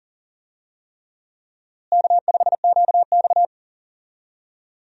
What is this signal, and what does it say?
Morse code, heard as a keyed tone, about 40 wpm, 700 Hz. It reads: K5QX
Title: USA Call Signs - 4 Characters - Repeated 3x - 40wpm